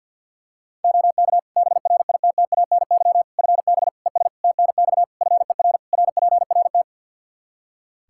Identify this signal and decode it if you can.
Morse code, heard as a keyed tone, about 51 wpm, 700 Hz. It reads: OK BRITTANY FB ES TNX FER RPRT
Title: K1USN SST QSO Practice 50wpm